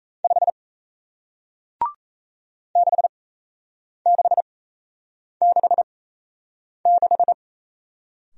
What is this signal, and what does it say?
Morse code, heard as a keyed tone, about 38 wpm, spaced as for 8 wpm, 700 Hz. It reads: 5 6666